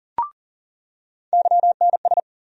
Morse code, heard as a keyed tone, about 40 wpm, 700 Hz. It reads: YNS
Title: USA Call Sign Postfixes - 3 Letters 40wpm